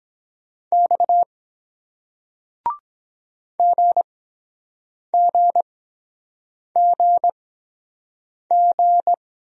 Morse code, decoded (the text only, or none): X GGGG